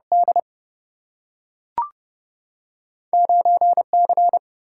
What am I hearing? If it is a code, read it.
Morse code, heard as a keyed tone, about 30 wpm, 700 Hz. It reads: D 9 C
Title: Single-Letter Single-Number 30wpm